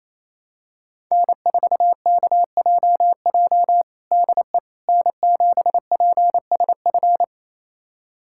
Morse code, heard as a keyed tone, about 28 wpm, 700 Hz. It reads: N4KJJ DE N7PSF
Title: USA Callsign DE Callsign - 5 Characters - 28wpm